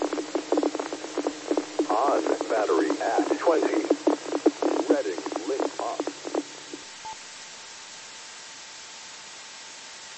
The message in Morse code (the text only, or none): none